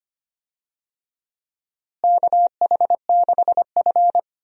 Morse code, heard as a keyed tone, about 25 wpm, 700 Hz. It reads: KH6F
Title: DXCC-100 Call Signs - 4 Characters - Repeated 2x - 25wpm